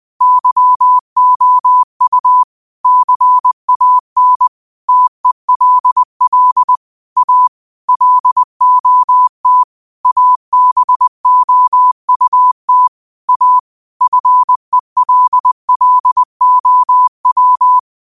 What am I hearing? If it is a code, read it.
Morse code, heard as a keyed tone, about 20 wpm, 1000 Hz. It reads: YOU CAN TELL A LOT ABOUT A FELLOW